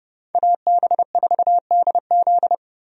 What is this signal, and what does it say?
Morse code, heard as a keyed tone, about 30 wpm, 700 Hz. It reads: AB4DZ
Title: USA Call Signs - 5 Characters - 30wpm